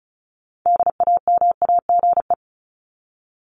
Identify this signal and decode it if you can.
Morse code, heard as a keyed tone, about 34 wpm, 700 Hz. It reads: DAMAGE
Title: Top 1001-1250 Words 35wpm